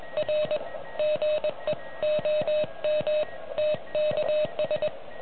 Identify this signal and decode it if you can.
Morse code, heard as a keyed tone, about 20 wpm, 610 Hz, with noise. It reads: R GE OM TKS